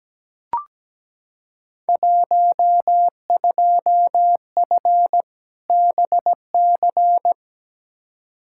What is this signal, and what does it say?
Morse code, heard as a keyed tone, about 17 wpm, 700 Hz. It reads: 12F BC
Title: Field Day Exchanges Encore - 17wpm